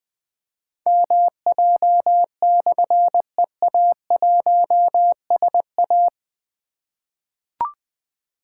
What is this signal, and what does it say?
Morse code, heard as a keyed tone, about 21 wpm, 700 Hz. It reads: MJ/EA1SA
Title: Difficult Call Signs 20wpm